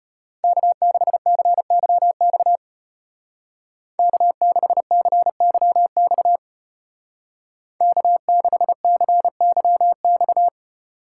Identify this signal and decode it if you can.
Morse code, heard as a keyed tone, about 32 wpm, 700 Hz. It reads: K6CYX K6CYX K6CYX